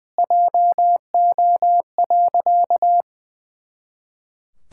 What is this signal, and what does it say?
Morse code, heard as a keyed tone, about 20 wpm, 700 Hz. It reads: JO.
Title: Pangram Challenge v2 - 20wpm